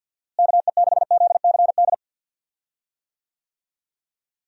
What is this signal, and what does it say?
Morse code, heard as a keyed tone, about 52 wpm, 700 Hz. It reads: KE6ZCD
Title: USA Call Signs - 6 Characters - Repeated 2x - 50wpm